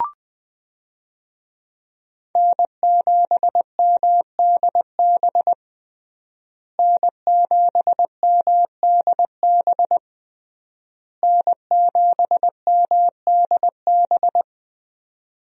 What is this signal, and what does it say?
Morse code, heard as a keyed tone, about 20 wpm, 700 Hz. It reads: N7MDB N7MDB N7MDB
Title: USA Call Signs - 5 Characters - Repeated 3x - 20wpm